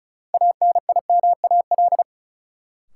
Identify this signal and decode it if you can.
Morse code, heard as a keyed tone, about 36 wpm, 700 Hz. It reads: ANIMAL
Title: Sentences from Top 1000 Words - Review Part 1 - New Word Drill - 35wpm